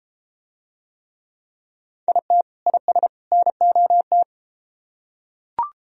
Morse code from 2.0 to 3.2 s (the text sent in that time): IT IS